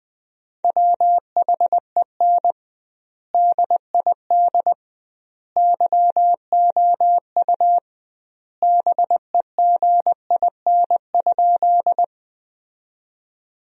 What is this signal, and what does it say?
Morse code, heard as a keyed tone, about 20 wpm, 700 Hz. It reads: WHEN DID YOU BEGIN?
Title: 2x Word-Spacing - Sentences from Top 300 Words - 20wpm